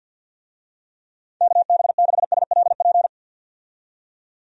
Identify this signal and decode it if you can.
Morse code, heard as a keyed tone, about 48 wpm, 700 Hz. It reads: KB6SLP
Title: USA Call Signs - 6 Characters - 50wpm